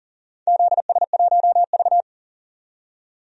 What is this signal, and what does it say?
Morse code, heard as a keyed tone, about 40 wpm, 700 Hz. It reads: ZS1V
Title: DXCC-100 Call Signs - 4 Characters - 40wpm